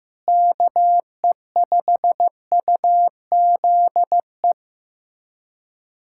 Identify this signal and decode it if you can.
Morse code, heard as a keyed tone, about 15 wpm, 700 Hz. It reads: KE5UZE
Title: USA Callsign DE Callsign - 6 Characters - 15wpm